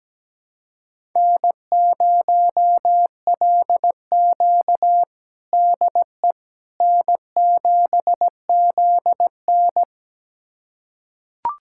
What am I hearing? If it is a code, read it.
Morse code, heard as a keyed tone, about 17 wpm, 700 Hz. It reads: N0LQ DE N7ZN
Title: USA Callsign DE Callsign - 4 Characters - 17wpm